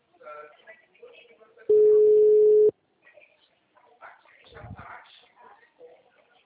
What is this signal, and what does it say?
Ringback tone (Europe, 425 Hz), with speech.